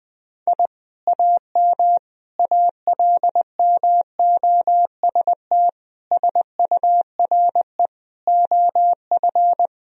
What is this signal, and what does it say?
Morse code, heard as a keyed tone, about 20 wpm, 700 Hz. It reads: I AM ALMOST SURE OF
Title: Sentences from Top 1000 Words - Review Part 4 - 20wpm